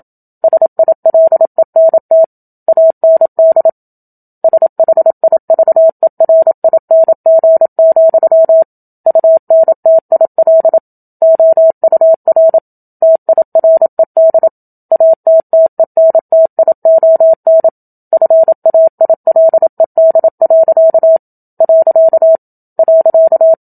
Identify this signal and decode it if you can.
Morse code, heard as a keyed tone, about 27 wpm, 650 Hz. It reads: SILENT AND SHIVERING, UNTIL OUR TIRED ATTENTION FAILED. .
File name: Book Two / 01-Chapter 1 - 27 WPM